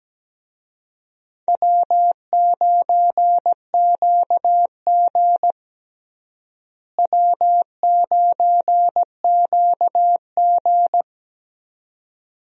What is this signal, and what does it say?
Morse code, heard as a keyed tone, about 17 wpm, 700 Hz. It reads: W9QG W9QG
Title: USA Call Signs - 4 Characters - Repeated 2x - 17wpm